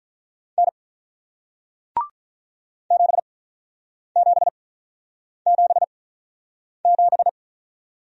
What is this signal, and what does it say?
Morse code, heard as a keyed tone, about 48 wpm, 700 Hz. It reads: N 7 7 7 7